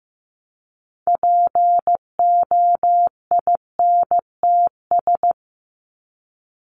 Morse code, heard as a keyed tone, about 15 wpm, 700 Hz. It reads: POINTS
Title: ES:S Suffix - Top 100 Words with Base Words 15wpm